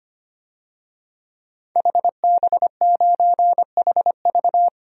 Morse code, heard as a keyed tone, about 25 wpm, 700 Hz. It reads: HB9HV
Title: DXCC-100 Call Signs - 5 Characters - 25wpm